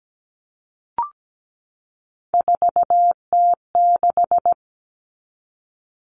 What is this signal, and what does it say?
Morse code, heard as a keyed tone, about 17 wpm, 690 Hz. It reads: 4T6
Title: Common Cut Numbers - 3-Digit 17wpm